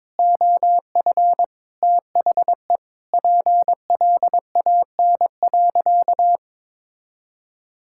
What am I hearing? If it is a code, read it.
Morse code, heard as a keyed tone, about 22 wpm, 700 Hz. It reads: OF THE PLAN.
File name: practice / 22wpm / Sentences with 8-Word Phrases - 22wpm